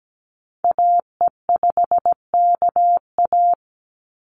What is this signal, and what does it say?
Morse code, heard as a keyed tone, about 17 wpm, 700 Hz. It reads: AE5KA
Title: NA SOTA Activator Callsigns 17wpm